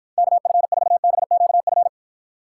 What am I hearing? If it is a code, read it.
Morse code, heard as a keyed tone, about 52 wpm, 700 Hz. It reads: KF4BQF